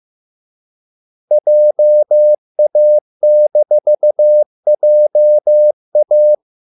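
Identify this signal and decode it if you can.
Morse code, heard as a keyed tone, about 15 wpm, 600 Hz. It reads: JA-JA
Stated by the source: Morse code at a slow 15 wpm from CW med Gnister.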